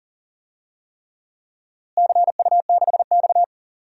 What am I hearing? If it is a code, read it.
Morse code, heard as a keyed tone, about 40 wpm, 700 Hz. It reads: CU6X
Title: DXCC-100 Call Signs - 4 Characters - Repeated 3x - 40wpm